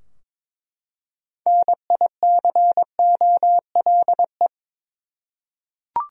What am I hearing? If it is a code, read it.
Morse code, heard as a keyed tone, about 22 wpm, 700 Hz. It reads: NICOLE
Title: USA Common Names - 22wpm